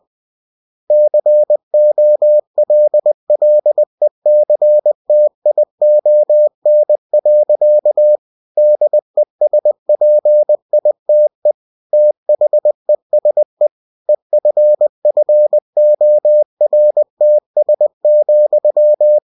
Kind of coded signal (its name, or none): Morse code